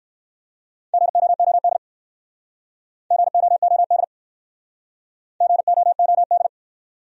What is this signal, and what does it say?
Morse code, heard as a keyed tone, about 65 wpm, 700 Hz. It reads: GOOD GOOD GOOD